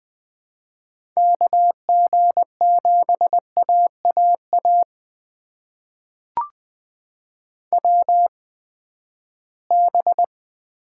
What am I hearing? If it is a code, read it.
Morse code, heard as a keyed tone, about 20 wpm, spaced as for 14 wpm, 700 Hz. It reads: KG7AAA WB
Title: USA Call Signs - 6 Characters - 8x Character Spacing - 20wpm